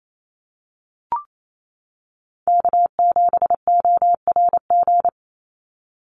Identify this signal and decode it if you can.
Morse code, heard as a keyed tone, about 28 wpm, 700 Hz. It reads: K7ORG
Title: NA SOTA Chaser Callsigns 28wpm